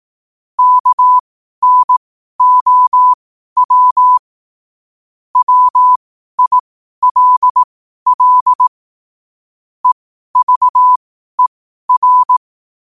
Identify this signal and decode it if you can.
Morse code, heard as a keyed tone, about 18 wpm, spaced as for 12 wpm, 1000 Hz. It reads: KNOW WILL EVER
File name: qotdfiles / 12wpm